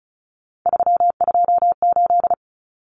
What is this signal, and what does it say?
Morse code, heard as a keyed tone, about 34 wpm, 700 Hz. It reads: 328
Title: Common Cut Numbers - 3-Digit 35wpm